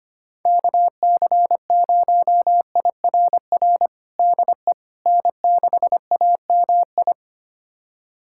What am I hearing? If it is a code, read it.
Morse code, heard as a keyed tone, about 25 wpm, 700 Hz. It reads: KC0IRR DE N6AMI
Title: USA Callsign DE Callsign 25wpm